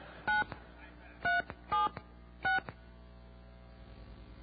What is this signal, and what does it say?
DTMF, heard as telephone touch tones, tones of about 160 ms. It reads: 9376